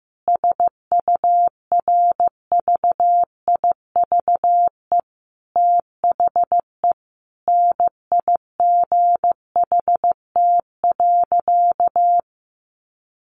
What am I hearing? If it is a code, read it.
Morse code, heard as a keyed tone, about 15 wpm, 700 Hz. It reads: SURVIVE THE NIGHT.
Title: Sentences from Top 1300 Words 15wpm